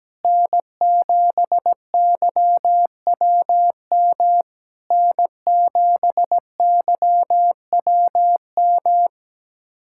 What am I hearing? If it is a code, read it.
Morse code, heard as a keyed tone, about 17 wpm, 700 Hz. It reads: N7YWM N7YWM
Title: SOTA QSO Practice 17wpm